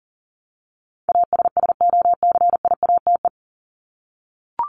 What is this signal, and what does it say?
Morse code, heard as a keyed tone, about 40 wpm, 700 Hz. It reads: ASSOCIATE
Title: Top 1751-2000 Words 40wpm